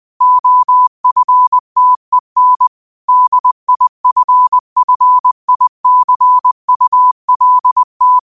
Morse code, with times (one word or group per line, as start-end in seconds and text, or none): 0.2-2.7 OFTEN
3.1-8.2 DIFFICULT